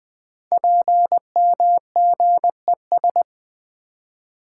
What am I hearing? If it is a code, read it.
Morse code, heard as a keyed tone, about 20 wpm, 700 Hz. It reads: PMGES
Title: Code Groups - 5 Letters - 20wpm